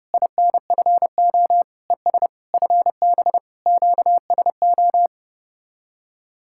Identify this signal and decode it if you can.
Morse code, heard as a keyed tone, about 33 wpm, 700 Hz. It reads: INFO ES FB QSO